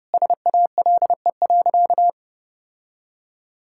Morse code, heard as a keyed tone, about 30 wpm, 700 Hz. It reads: SALE.
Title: Sentences from Top 1600 Words 30wpm